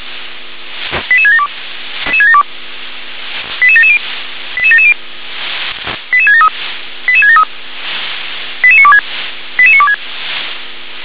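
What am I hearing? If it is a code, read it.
Five-tone selective call (ZVEI), heard as quick runs of five tones, tones of about 70 ms. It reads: 88672 88770 88770 88673 88673 88926 88926
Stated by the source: ZVEI five-tone selective call received on an SDR.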